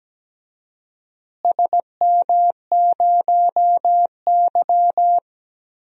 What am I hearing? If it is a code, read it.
Morse code, heard as a keyed tone, about 17 wpm, 700 Hz. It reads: SM0Y
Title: DXCC-100 Call Signs - 4 Characters - Repeated 2x - 17wpm